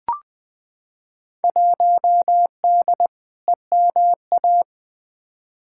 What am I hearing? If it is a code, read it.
Morse code, heard as a keyed tone, about 20 wpm, 700 Hz. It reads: 1D EMA
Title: Field Day Exchanges - 20wpm